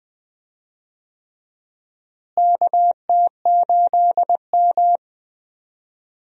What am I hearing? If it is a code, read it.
Morse code, heard as a keyed tone, about 20 wpm, 700 Hz. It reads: KT8M
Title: USA Call Signs - 4 Characters - Repeated 3x - 20wpm